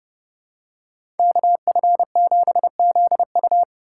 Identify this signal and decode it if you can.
Morse code, heard as a keyed tone, about 30 wpm, 700 Hz. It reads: KF7ZU
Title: USA Call Signs - 4x Character Spacing - 30wpm